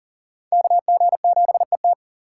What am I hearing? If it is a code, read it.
Morse code, heard as a keyed tone, about 50 wpm, 700 Hz. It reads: KG7ET